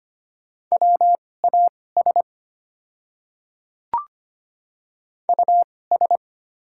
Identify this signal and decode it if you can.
Morse code, heard as a keyed tone, about 25 wpm, spaced as for 18 wpm, 700 Hz. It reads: WAS US
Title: 2x Character Spacing - Top 200 Words - 25wpm